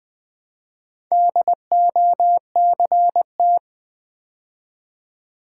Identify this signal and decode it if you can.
Morse code, heard as a keyed tone, about 20 wpm, 690 Hz. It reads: DOCT
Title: Build A Word - 2 Letters At A Time - 6-Letter Words - 20wpm